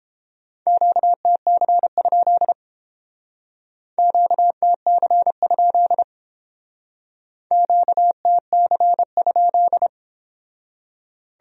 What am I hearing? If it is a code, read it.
Morse code, heard as a keyed tone, about 29 wpm, 700 Hz. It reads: QTC? QTC? QTC?